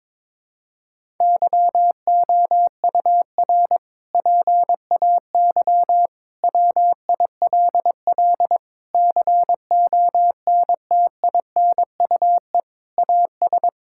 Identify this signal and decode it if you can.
Morse code, heard as a keyed tone, about 22 wpm, 700 Hz. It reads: YOUR PAY WILL CONTINUE AS
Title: Sentences from Top 1000 Words - Review Part 6 - 22wpm